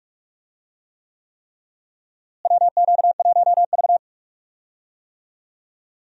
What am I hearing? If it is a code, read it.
Morse code, heard as a keyed tone, about 41 wpm, 700 Hz. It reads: WQ1V
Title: USA Call Signs - 4 Characters - Repeated 3x - 45wpm